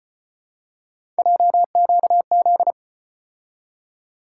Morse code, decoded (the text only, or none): JQZ